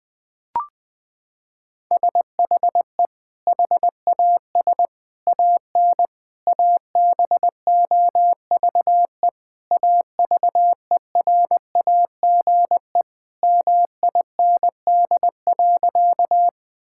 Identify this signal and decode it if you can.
Morse code, heard as a keyed tone, about 20 wpm, 700 Hz. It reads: SHE HAS AN ABOVE AVERAGE MIND.
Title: Sentences from Top 1000 Words - Review Part 6 - 20wpm